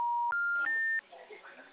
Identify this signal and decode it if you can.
Special information tone, with speech.
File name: out- -2018- - - .84